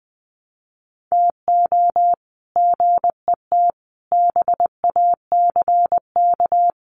Morse code, heard as a keyed tone, about 20 wpm, 700 Hz. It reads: TO GET BACK
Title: Sets of 5 Words Encore 20wpm